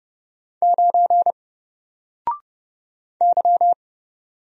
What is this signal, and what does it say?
Morse code, heard as a keyed tone, about 30 wpm, 700 Hz. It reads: 9 Y
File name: Single-Letter-Number-30wpm